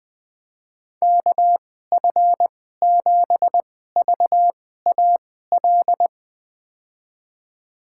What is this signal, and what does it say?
Morse code, heard as a keyed tone, about 20 wpm, spaced as for 14 wpm, 700 Hz. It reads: KF7VAL
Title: USA Call Signs - 6 Characters - 2x Character Spacing - 20wpm